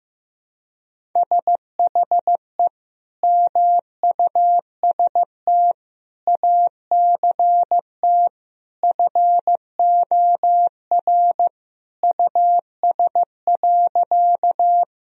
Morse code, heard as a keyed tone, about 15 wpm, 700 Hz. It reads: SHE MUST ACT FOR US.
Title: Sentences from Top 1000 Words - Review Part 1 - 15wpm